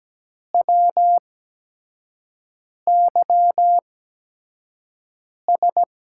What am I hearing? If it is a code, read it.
Morse code, heard as a keyed tone, about 17 wpm, spaced as for 5 wpm, 700 Hz. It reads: WYS
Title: USA Call Signs - 8x Character Spacing - 17wpm